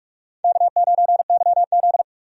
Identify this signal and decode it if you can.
Morse code, heard as a keyed tone, about 45 wpm, 700 Hz. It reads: K9YZ
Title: USA Call Signs - 4 Characters - 45wpm